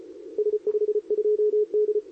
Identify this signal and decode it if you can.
Morse code, heard as a keyed tone, about 31 wpm, 400 Hz. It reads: S52D
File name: CQWWCW2023 / SP